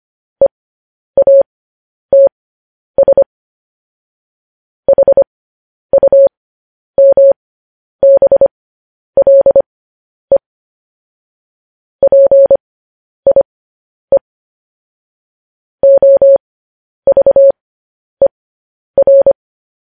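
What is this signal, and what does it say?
Morse code, heard as a keyed tone, about 25 wpm, spaced as for 10 wpm, 550 Hz. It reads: EATS HUMBLE PIE OVER
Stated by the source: Morse code transmission of news headlines.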